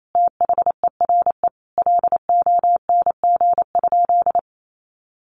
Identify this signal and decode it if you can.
Morse code, heard as a keyed tone, about 28 wpm, 700 Hz. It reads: THERE LONG?
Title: Sentences from Top 1200 Words 28wpm